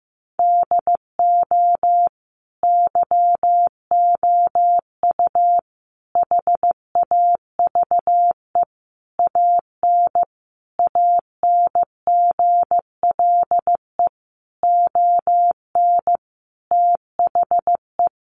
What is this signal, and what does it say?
Morse code, heard as a keyed tone, about 15 wpm, 700 Hz. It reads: DO YOU HAVE AN ANGLE ON THE
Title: Sentences from Top 1500 Words 15wpm